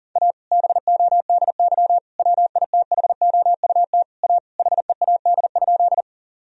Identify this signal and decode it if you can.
Morse code, heard as a keyed tone, about 37 wpm, 700 Hz. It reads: A BODY WITHOUT A HEAD?
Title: Sentences from Top 1000 Words - Review Part 3 - 40wpm